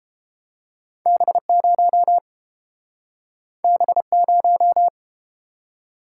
Morse code, heard as a keyed tone, about 31 wpm, 700 Hz. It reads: B0 B0